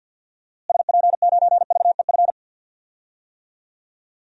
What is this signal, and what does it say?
Morse code, heard as a keyed tone, about 50 wpm, 700 Hz. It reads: SP9VEF
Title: DXCC-100 Call Signs - 6 Characters - Repeated 2x - 50wpm